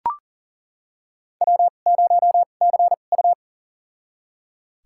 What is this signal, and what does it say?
Morse code, heard as a keyed tone, about 40 wpm, spaced as for 29 wpm, 700 Hz. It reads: W0CU
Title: USA Call Signs - 4 Characters - 2x Character Spacing - 40wpm